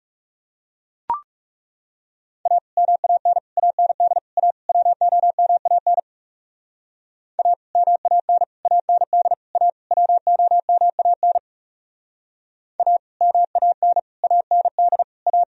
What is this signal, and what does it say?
Morse code, heard as a keyed tone, about 36 wpm, 700 Hz. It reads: A MAN AND A WOMAN A MAN AND A WOMAN A MAN AND A
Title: Speed-Racing - Sets of 5 Words 30wpm